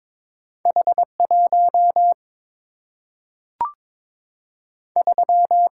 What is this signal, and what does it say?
Morse code, heard as a keyed tone, about 22 wpm, 700 Hz. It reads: H 1 3
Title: Single-Letter Single-Number 22wpm